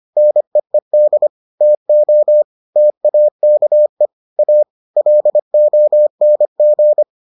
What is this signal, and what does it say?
Morse code, heard as a keyed tone, about 25 wpm, 600 Hz. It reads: NEED TO TAKE A LONG